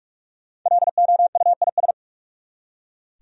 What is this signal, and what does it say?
Morse code, heard as a keyed tone, about 46 wpm, 700 Hz. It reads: LOUIS
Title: USA Common Names - Max 5 Letters - 45wpm